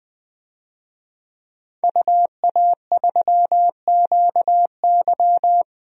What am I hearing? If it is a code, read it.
Morse code, heard as a keyed tone, about 21 wpm, 700 Hz. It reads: UA3QY
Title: DXCC-100 Call Signs - 5 Characters - 20wpm